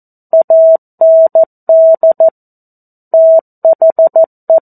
Morse code, heard as a keyed tone, about 14 wpm, 650 Hz. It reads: AND THE
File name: 02-Chapter-02 14 WPM